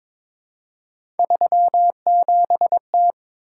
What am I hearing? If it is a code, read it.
Morse code, heard as a keyed tone, about 22 wpm, 700 Hz. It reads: 37T
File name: practice / 22wpm / Three Letter-Number Combinations - 22wpm